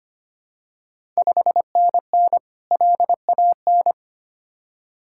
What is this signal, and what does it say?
Morse code, heard as a keyed tone, about 25 wpm, 700 Hz. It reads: 5NN LAN
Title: Ontario QSO Party Exchange 25wpm